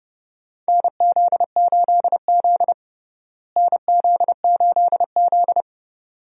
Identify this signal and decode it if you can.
Morse code, heard as a keyed tone, about 30 wpm, 700 Hz. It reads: NZ8Z NZ8Z
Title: USA Call Signs - 4 Characters - Repeated 2x - 30wpm